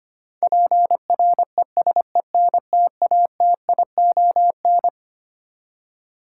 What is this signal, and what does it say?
Morse code, heard as a keyed tone, about 25 wpm, 700 Hz. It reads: PRESENTATION
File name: Twelve Letter Words - 25wpm